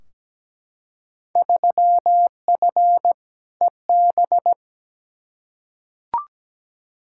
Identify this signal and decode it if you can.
Morse code, heard as a keyed tone, about 17 wpm, 700 Hz. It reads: 3F EB
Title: Field Day Exchanges - 17wpm